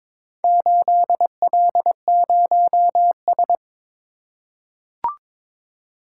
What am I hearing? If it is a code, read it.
Morse code, heard as a keyed tone, about 22 wpm, 700 Hz. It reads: 8L0S
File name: practice / 22wpm / Four Letter-Number Combinations - 22wpm